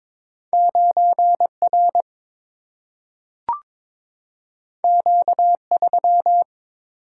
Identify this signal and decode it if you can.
Morse code, heard as a keyed tone, about 22 wpm, 700 Hz. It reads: 9 R Q 3
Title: Single-Letter Single-Number 22wpm